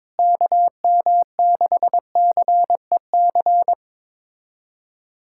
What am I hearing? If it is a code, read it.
Morse code, heard as a keyed tone, about 22 wpm, 700 Hz. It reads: KM6CEC
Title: USA Callsign DE Callsign 22wpm